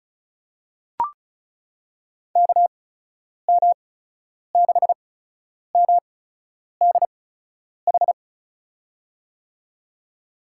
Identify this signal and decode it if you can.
Morse code, heard as a keyed tone, about 36 wpm, spaced as for 10 wpm, 700 Hz. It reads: KM6MDH
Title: USA Call Signs - 6 Characters - 8x Character Spacing - 35wpm